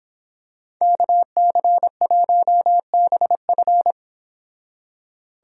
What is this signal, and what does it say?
Morse code, heard as a keyed tone, about 26 wpm, 700 Hz. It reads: KC1BF